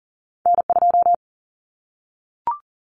Morse code, heard as a keyed tone, about 40 wpm, 700 Hz. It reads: N2
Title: Common Cut Numbers - 2-Digit 40wpm